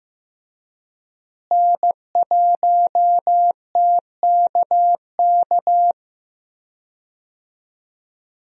Morse code, heard as a keyed tone, about 15 wpm, 700 Hz. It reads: N1TKK
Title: USA Call Signs - 5 Characters - Repeated 3x - 15wpm